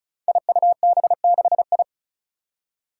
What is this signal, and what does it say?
Morse code, heard as a keyed tone, about 34 wpm, 700 Hz. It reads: IUB6I
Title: Five Challenging Letter-Number Combinations 35wpm